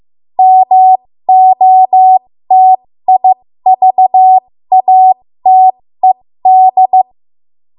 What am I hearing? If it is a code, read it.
Morse code, heard as a keyed tone, about 15 wpm, spaced as for 13 wpm, 750 Hz. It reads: MOTIVATED
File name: Morse / 13WPM